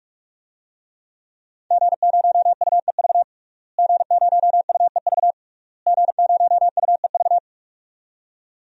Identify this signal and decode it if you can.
Morse code, heard as a keyed tone, about 43 wpm, 700 Hz. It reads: G0UEV G0UEV G0UEV
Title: DXCC-100 Call Signs - 5 Characters - Repeated 3x - 45wpm